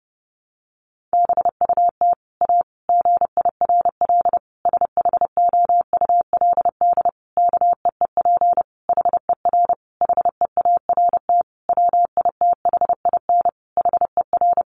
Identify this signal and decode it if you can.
Morse code, heard as a keyed tone, about 30 wpm, 700 Hz. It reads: BUT A GIRL SHOULD KEEP HER HEART WITHIN HER
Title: Sentences from Top 1200 Words 30wpm